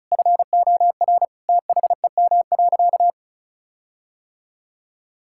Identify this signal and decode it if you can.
Morse code, heard as a keyed tone, about 36 wpm, 700 Hz. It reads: FOR THEM.
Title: Sentences from Top 1600 Words 35wpm